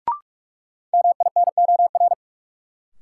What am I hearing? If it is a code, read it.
Morse code, heard as a keyed tone, about 46 wpm, 700 Hz. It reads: MINOR
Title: Sentences from Top 1500 Words - New Word Drill - 45wpm